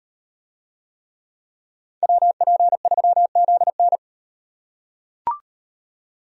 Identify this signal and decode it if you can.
Morse code, heard as a keyed tone, about 36 wpm, 700 Hz. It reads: WP3ZN